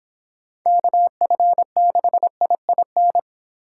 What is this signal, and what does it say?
Morse code, heard as a keyed tone, about 29 wpm, 700 Hz. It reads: KF6IIN